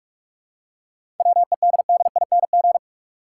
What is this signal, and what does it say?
Morse code, heard as a keyed tone, about 46 wpm, 700 Hz. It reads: WEDDING
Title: Seven Letter Words - 45wpm